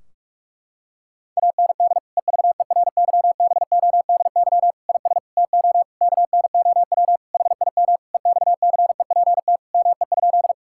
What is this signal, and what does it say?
Morse code, heard as a keyed tone, about 46 wpm, 700 Hz. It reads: AND EVERYBODY IS TO KNOW HIM EXCEPT ME?
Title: Sentences from Top 1600 Words 45wpm